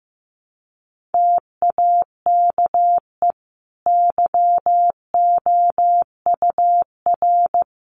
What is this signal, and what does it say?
Morse code, heard as a keyed tone, about 14 wpm, 700 Hz. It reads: TAKE YOUR
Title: Sentences from Top 1500 Words 15wpm